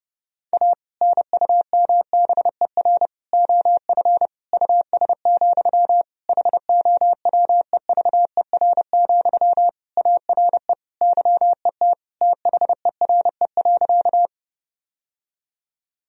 Morse code, heard as a keyed tone, about 31 wpm, 700 Hz. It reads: A NUMBER OF US, HOWEVER, ARE YET THERE.
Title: Sentences from Top 1000 Words - Review Part 1 - 30wpm